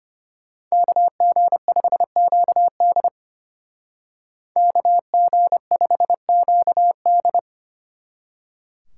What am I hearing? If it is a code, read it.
Morse code, heard as a keyed tone, about 31 wpm, 700 Hz. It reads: KG5QD KG5QD
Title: Speed Racing - USA 2x2 Call Signs 25wpm